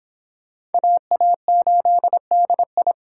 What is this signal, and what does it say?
Morse code, heard as a keyed tone, about 25 wpm, 700 Hz. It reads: AA8DI
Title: Speed Racing - USA 2x2 Call Signs 22wpm